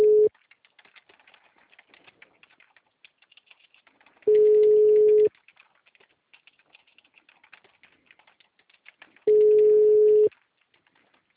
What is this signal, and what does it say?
Ringback tone (Europe, 425 Hz).